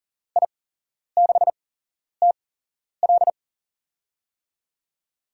Morse code, heard as a keyed tone, about 42 wpm, spaced as for 11 wpm, 700 Hz. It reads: I6TL
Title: USA Call Signs - 8x Character Spacing - 40wpm